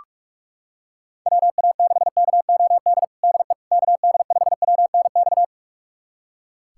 Morse code, heard as a keyed tone, about 46 wpm, 700 Hz. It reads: WA6KOD DE KD5WNX
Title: USA Callsign DE Callsign - 6 Characters - 45wpm